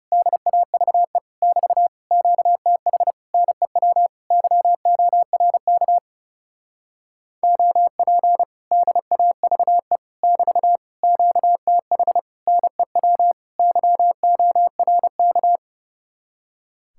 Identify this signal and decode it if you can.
Morse code, heard as a keyed tone, about 30 wpm, 700 Hz. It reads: DAVE <BT> QTH NEW YORK OP DAVE <BT> QTH NEW YORK
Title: Speed-Racing Common QSO Phrases - 30wpm